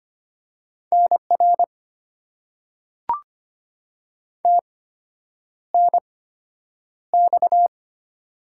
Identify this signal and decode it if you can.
Morse code, heard as a keyed tone, about 25 wpm, spaced as for 9 wpm, 690 Hz. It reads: NR TNX
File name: Top 90 QSO Elements - 8x Character Spacing - 25wpm